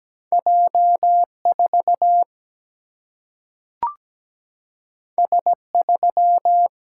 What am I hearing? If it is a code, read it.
Morse code, heard as a keyed tone, about 17 wpm, 700 Hz. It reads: J4 S3
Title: Single-Letter Single-Number 17wpm